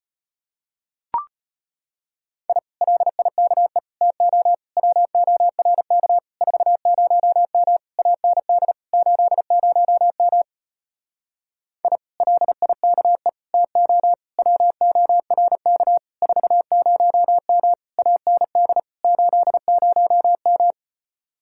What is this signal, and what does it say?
Morse code, heard as a keyed tone, about 33 wpm, 700 Hz. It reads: I LIKE TO WORK 40M AND 80M I LIKE TO WORK 40M AND 80M